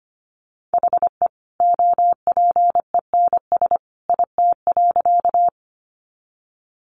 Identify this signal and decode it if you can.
Morse code, heard as a keyed tone, about 25 wpm, 700 Hz. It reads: HE OPENS IT.
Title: Additional Easy Phrases 25wpm